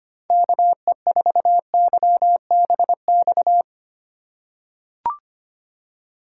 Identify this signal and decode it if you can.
Morse code, heard as a keyed tone, about 25 wpm, 700 Hz. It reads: KE4YBX
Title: USA Call Signs - 8x Character Spacing - 25wpm